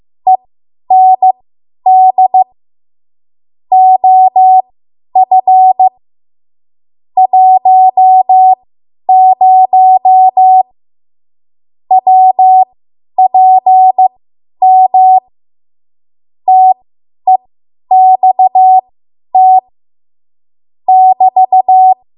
Morse code, heard as a keyed tone, about 15 wpm, spaced as for 10 wpm, 750 Hz. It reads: END OF 10 WPM TEXT <BT>
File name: Archive / 10 WPM / 10WPM